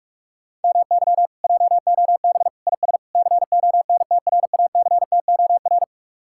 Morse code, heard as a keyed tone, about 58 wpm, 700 Hz. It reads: MY JOB IS CONTRACTOR